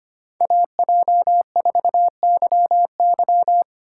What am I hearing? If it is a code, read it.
Morse code, heard as a keyed tone, about 25 wpm, 700 Hz. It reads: AJ4YY
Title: USA Call Signs - 5 Characters - 2x Character Spacing - 25wpm